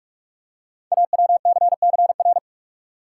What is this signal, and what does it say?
Morse code, heard as a keyed tone, about 44 wpm, 690 Hz. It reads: AWCCR
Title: Code Groups - 5 Letters - 45wpm